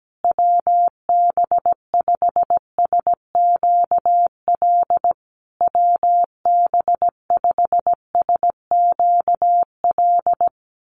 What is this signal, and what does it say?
Morse code, heard as a keyed tone, about 17 wpm, 700 Hz. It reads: WB5SQL WB5SQL
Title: USA Call Signs Repeated 3x 17wpm